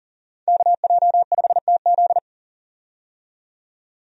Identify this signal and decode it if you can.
Morse code, heard as a keyed tone, about 39 wpm, 700 Hz. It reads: KJ5TZ
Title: USA Call Signs - 5 Characters - 40wpm